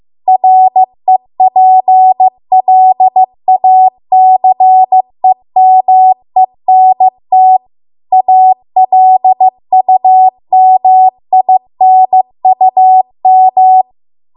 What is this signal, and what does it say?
Morse code, heard as a keyed tone, about 15 wpm, 750 Hz. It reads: REPLACEMENT ALUMINUM
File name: Archive / 15 WPM / 15WPM